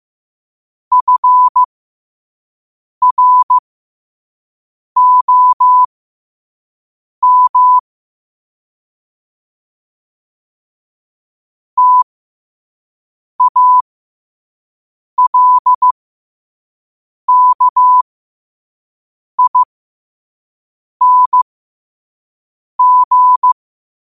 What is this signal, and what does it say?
Morse code, heard as a keyed tone, about 15 wpm, spaced as for 5 wpm, 1000 Hz. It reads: FROM TALKING
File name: qotdfiles / 5wpm